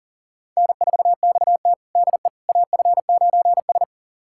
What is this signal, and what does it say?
Morse code, heard as a keyed tone, about 40 wpm, 700 Hz. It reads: N4XT DE AF9S
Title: USA Callsign DE Callsign - 4 Characters - 40wpm